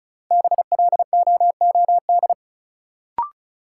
Morse code, heard as a keyed tone, about 36 wpm, 700 Hz. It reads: BLOOD
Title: Sentences from Top 1000 Words - Review Part 2 - New Word Drill - 35wpm